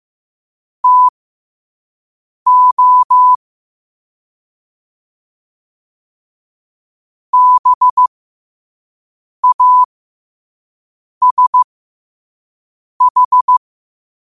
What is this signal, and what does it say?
Morse code, heard as a keyed tone, about 15 wpm, spaced as for 5 wpm, 1000 Hz. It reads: TO BASH
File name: qotdfiles / 5wpm